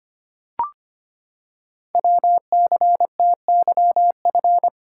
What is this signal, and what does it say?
Morse code, heard as a keyed tone, about 25 wpm, 700 Hz. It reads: WCTYF
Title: Code Groups - 5 Letter:Numbers - 25wpm